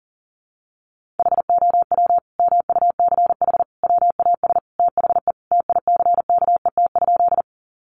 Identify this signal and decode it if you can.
Morse code, heard as a keyed tone, about 40 wpm, 700 Hz. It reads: HOW MUCH WAS THE TICKET?
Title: Sentences from Top 1400 Words 40wpm